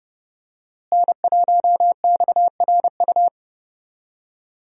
Morse code, heard as a keyed tone, about 30 wpm, 700 Hz. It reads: N1XRU
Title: USA Call Signs - 5 Characters - Repeated 2x - 30wpm